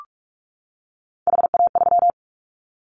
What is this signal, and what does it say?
Morse code, heard as a keyed tone, about 44 wpm, 700 Hz. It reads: HA3
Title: Callsign Prefixes for Top 50 DXCC - 45wpm